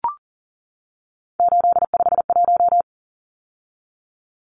Morse code, heard as a keyed tone, about 40 wpm, 700 Hz. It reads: 851
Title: Numbers - 3-Digit 40wpm